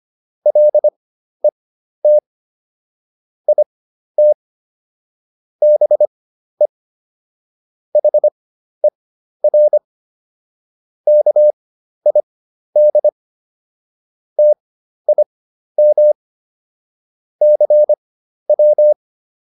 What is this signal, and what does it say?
Morse code, heard as a keyed tone, about 25 wpm, spaced as for 12 wpm, 600 Hz. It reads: LET IT BE HER KID TIM CW